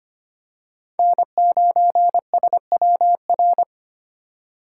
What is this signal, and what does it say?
Morse code, heard as a keyed tone, about 25 wpm, 700 Hz. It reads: N9SWR
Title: USA Call Signs - 5 Characters - 4x Character Spacing - 25wpm